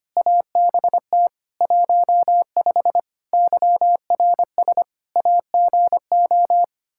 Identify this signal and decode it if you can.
Morse code, heard as a keyed tone, about 25 wpm, 700 Hz. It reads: ABT 15 YRS AGO
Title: Practice QSO - Exchange 3 - More Info - 25wpm